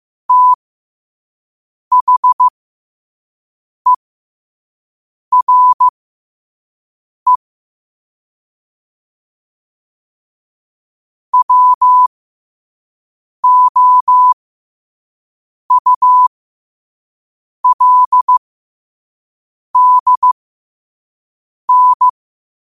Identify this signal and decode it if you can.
Morse code, heard as a keyed tone, about 15 wpm, spaced as for 5 wpm, 1000 Hz. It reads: THERE WOULDN